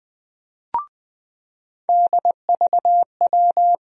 Morse code, heard as a keyed tone, about 20 wpm, 700 Hz. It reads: DVW
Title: USA Call Sign Postfixes - 3 Letters 20wpm